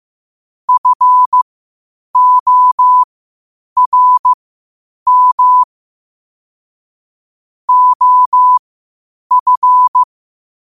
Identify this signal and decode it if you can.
Morse code, heard as a keyed tone, about 15 wpm, spaced as for 8 wpm, 1000 Hz. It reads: FORM OF
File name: qotdfiles / 8wpm